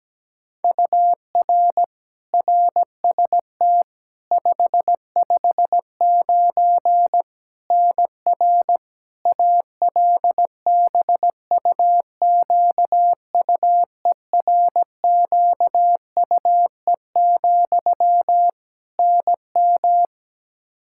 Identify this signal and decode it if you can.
Morse code, heard as a keyed tone, about 17 wpm, 700 Hz. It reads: UR RST 559 NR ALBUQUERQUE, NM
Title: Practice QSO - Exchange 1 - Beginning - 17wpm